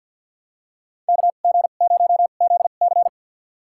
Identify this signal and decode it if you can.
Morse code, heard as a keyed tone, about 53 wpm, spaced as for 37 wpm, 700 Hz. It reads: KG0ZC
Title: USA Call Signs - 5 Characters - 2x Character Spacing - 50wpm